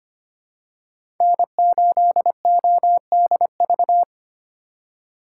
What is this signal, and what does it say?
Morse code, heard as a keyed tone, about 25 wpm, 700 Hz. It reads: N8ODV
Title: USA Call Signs - 2x Character Spacing - 25wpm